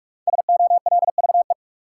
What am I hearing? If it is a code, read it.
Morse code, heard as a keyed tone, about 47 wpm, 700 Hz. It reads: SOLVE
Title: Five Letter Words - 45wpm